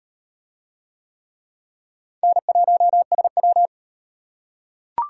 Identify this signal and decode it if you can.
Morse code, heard as a keyed tone, about 36 wpm, 700 Hz. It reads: N1SW